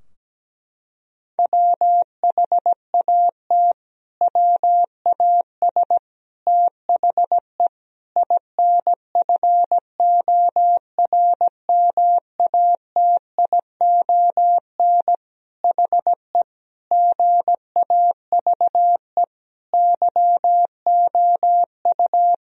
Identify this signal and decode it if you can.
Morse code, heard as a keyed tone, about 17 wpm, 700 Hz. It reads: WHAT WAS THE INFORMATION HE GAVE YOU
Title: Sentences from Top 1000 Words - Review Part 3 - 17wpm